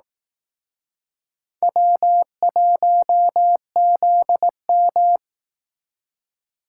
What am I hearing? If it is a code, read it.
Morse code, heard as a keyed tone, about 18 wpm, 700 Hz. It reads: W1ZM
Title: Speed Racing - USA 1x2 Call Signs 15wpm